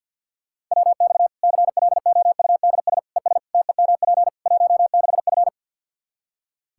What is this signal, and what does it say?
Morse code, heard as a keyed tone, about 46 wpm, 700 Hz. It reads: WX CLOUDS ES TEMP 16F
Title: Practice QSO - Exchange 2 - Info - 50wpm